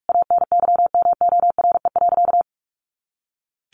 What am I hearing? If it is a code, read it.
Morse code, heard as a keyed tone, about 44 wpm, 700 Hz. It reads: ANYMORE.
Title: Easy Phrases 45wpm